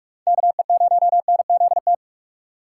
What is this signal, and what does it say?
Morse code, heard as a keyed tone, about 42 wpm, 700 Hz. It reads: KE0NZT
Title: USA Call Signs - 6 Characters - 2x Character Spacing - 45wpm